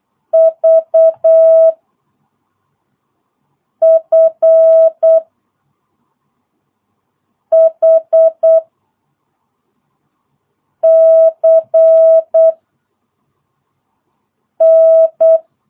Morse code, heard as a keyed tone, about 8 wpm, 660 Hz. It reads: V F H C N
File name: jh8cbh / 2022cw